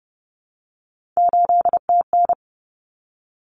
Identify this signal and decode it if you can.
Morse code, heard as a keyed tone, about 30 wpm, 700 Hz. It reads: 8TN
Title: Common Cut Numbers - 3-Digit 30wpm